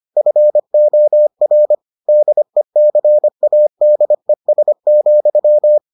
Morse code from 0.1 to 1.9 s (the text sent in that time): FOR